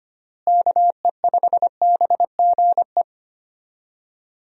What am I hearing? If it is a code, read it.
Morse code, heard as a keyed tone, about 25 wpm, 700 Hz. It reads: KE5BGE
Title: USA Call Signs - 6 Characters - Repeated 2x - 25wpm